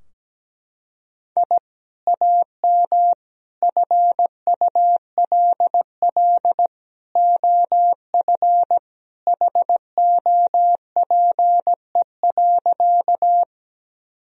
Morse code, heard as a keyed tone, about 17 wpm, 700 Hz. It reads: I AM FULL OF HOPE.
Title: Sentences from Top 1000 Words - Review Part 3 - 17wpm